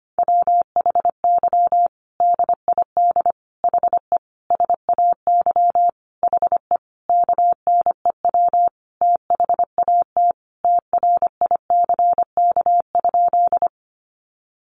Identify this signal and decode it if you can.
Morse code, heard as a keyed tone, about 25 wpm, 700 Hz. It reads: WHY DID HE SAY HE KNEW THAT TRICK?
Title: Sentences from Top 1500 Words 25wpm